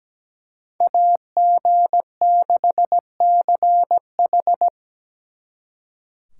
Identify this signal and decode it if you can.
Morse code, heard as a keyed tone, about 20 wpm, 700 Hz. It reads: AG6CH
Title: Speed Racing - USA Call Signs 17wpm